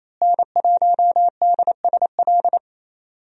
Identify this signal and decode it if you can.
Morse code, heard as a keyed tone, about 27 wpm, 700 Hz. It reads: N1DSL
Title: USA Call Signs - 5 Characters - Repeated 2x - 28wpm